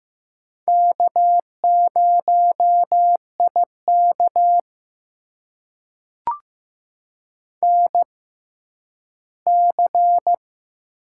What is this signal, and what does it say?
Morse code, heard as a keyed tone, about 15 wpm, spaced as for 10 wpm, 700 Hz. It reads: K0IK NC
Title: USA Call Signs - 4 Characters - 6x Character Spacing - 15wpm